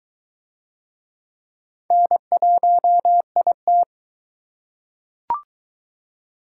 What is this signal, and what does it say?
Morse code, heard as a keyed tone, about 23 wpm, 700 Hz. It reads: N1IT